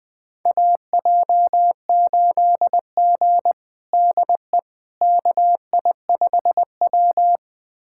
Morse code, heard as a keyed tone, about 20 wpm, 700 Hz. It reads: AJ8G DE KI5W
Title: USA Callsign DE Callsign - 4 Characters - 20wpm